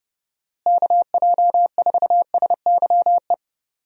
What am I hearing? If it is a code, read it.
Morse code, heard as a keyed tone, about 32 wpm, 700 Hz. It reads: KJ4SYE
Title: USA Call Signs - 6 Characters - Repeated 2x - 30wpm